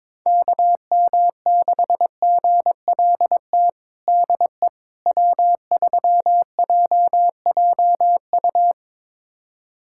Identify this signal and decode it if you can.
Morse code, heard as a keyed tone, about 22 wpm, 700 Hz. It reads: KM6GLT DE W3JJU
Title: USA Callsign DE Callsign 22wpm